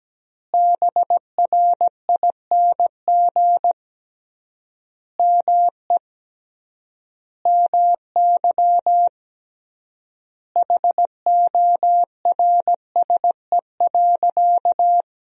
Morse code, heard as a keyed tone, about 17 wpm, 700 Hz. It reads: BRING ME MY HORSE.
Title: 3x Word-Spacing - Sentences from Top 500 Words - 17wpm